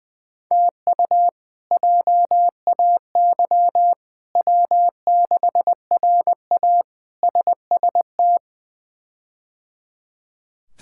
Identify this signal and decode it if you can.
Morse code, heard as a keyed tone, about 21 wpm, 700 Hz. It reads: TU JAY W6RA SST
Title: K1USN SST QSO Practice - 20wpm